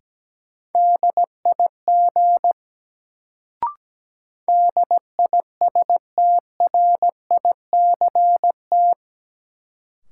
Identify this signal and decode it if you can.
Morse code, heard as a keyed tone, about 17 wpm, 700 Hz. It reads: DIG DISTRICT
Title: Sentences from Top 1600 Words - New Word Drill Alt - 17wpm